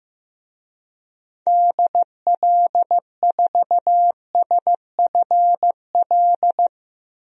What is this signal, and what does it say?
Morse code, heard as a keyed tone, about 15 wpm, 700 Hz. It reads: DL4SFL
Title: DXCC-100 Call Signs - 6 Characters - 15wpm